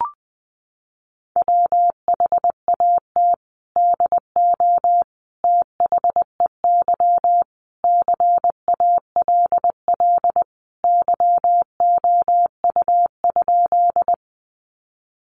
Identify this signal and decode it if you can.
Morse code, heard as a keyed tone, about 20 wpm, 700 Hz. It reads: WHAT DO THEY CALL YOU?
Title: Sentences from Top 100 Words 20wpm